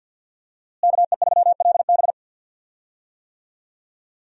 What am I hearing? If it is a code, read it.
Morse code, heard as a keyed tone, about 46 wpm, 690 Hz. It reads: KE3LB
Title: USA Call Signs - 5 Characters - Repeated 3x - 50wpm